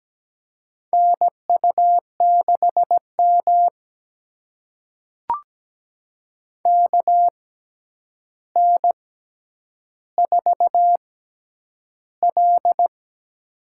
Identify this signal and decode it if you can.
Morse code, heard as a keyed tone, about 17 wpm, spaced as for 9 wpm, 700 Hz. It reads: NU6M KN4L
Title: USA Call Signs - 4 Characters - 6x Character Spacing - 17wpm